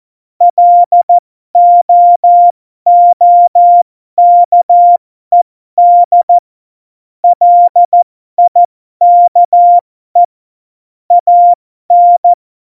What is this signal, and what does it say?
Morse code, heard as a keyed tone, about 14 wpm, spaced as for 12 wpm, 700 Hz. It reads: LOOKED LIKE AN
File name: Morsekurs / Witze